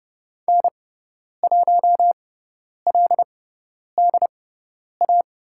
Morse code, heard as a keyed tone, about 30 wpm, spaced as for 10 wpm, 700 Hz. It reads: N1LDA
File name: Lesson 18g - Letter m - Call Signs 30wpm